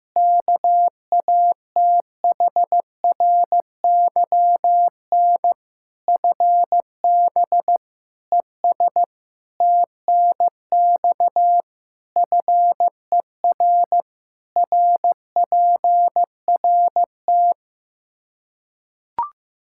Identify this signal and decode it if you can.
Morse code, heard as a keyed tone, about 15 wpm, 700 Hz. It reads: KATHRYN FB ES TNX FER RPRT
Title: Practice QSO - Exchange 2 - Info - 15wpm